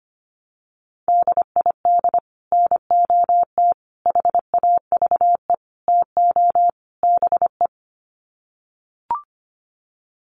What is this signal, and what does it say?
Morse code, heard as a keyed tone, about 25 wpm, 700 Hz. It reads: DID NOT HAVE TO BE
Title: Sets of 5 Words Encore2 25wpm